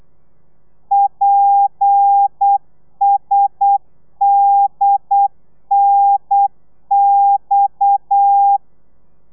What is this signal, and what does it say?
Morse code, heard as a keyed tone, about 8 wpm, 800 Hz. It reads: PSDNX